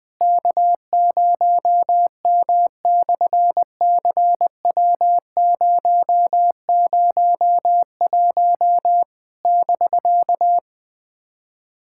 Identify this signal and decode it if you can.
Morse code, heard as a keyed tone, about 20 wpm, 700 Hz. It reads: K0M/CW001 <BK>
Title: SOTA QSO Practice 20wpm